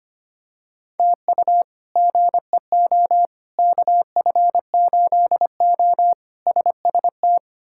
Morse code, heard as a keyed tone, about 25 wpm, 700 Hz. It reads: TU GEO KF8O SST
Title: K1USN SST QSO Practice - 25wpm